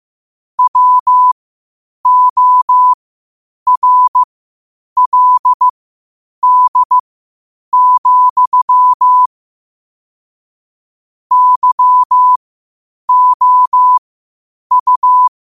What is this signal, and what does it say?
Morse code, heard as a keyed tone, about 15 wpm, spaced as for 8 wpm, 1000 Hz. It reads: WORLD, YOU
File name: qotdfiles / 8wpm